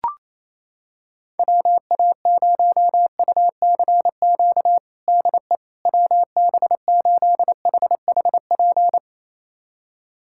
Morse code, heard as a keyed tone, about 27 wpm, 700 Hz. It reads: WA0UCQ DE WB8HHP
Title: USA Callsign DE Callsign - 6 Characters - 28wpm